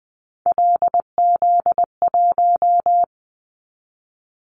Morse code, heard as a keyed tone, about 20 wpm, 700 Hz. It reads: LZ1
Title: Callsign Prefixes for Top 50 DXCC - 20wpm